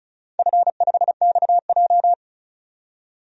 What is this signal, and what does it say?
Morse code, heard as a keyed tone, about 36 wpm, 700 Hz. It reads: F5XJ
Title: DXCC-100 Call Signs - 4 Characters - 35wpm